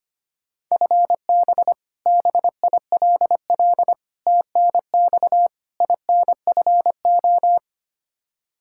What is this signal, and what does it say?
Morse code, heard as a keyed tone, about 25 wpm, 700 Hz. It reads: FB BILL TNX INFO
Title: Practice QSO - Exchange 3 - More Info - 25wpm